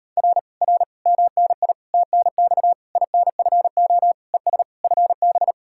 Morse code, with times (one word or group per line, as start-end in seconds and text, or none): 0.2-0.4 R
0.6-0.8 R
1.1-1.7 MNI
1.9-2.7 TNX
3.0-4.1 INFO
4.3-4.6 ES
4.8-5.5 FB